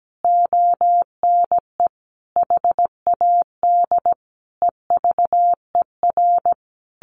Morse code, heard as a keyed tone, about 17 wpm, 700 Hz. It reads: ONE HAD EVER
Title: Sentences from Top 1200 Words 17wpm